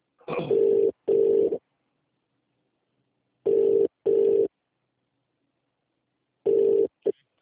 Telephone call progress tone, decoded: ringback tone